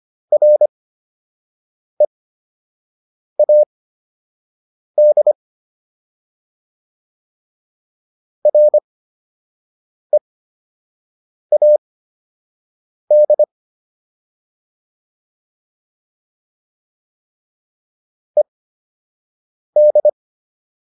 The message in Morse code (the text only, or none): READ READ ED